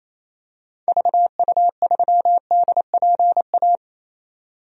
Morse code, heard as a keyed tone, about 28 wpm, 700 Hz. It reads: VU3DPA
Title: DXCC-100 Call Signs - 6 Characters - 28wpm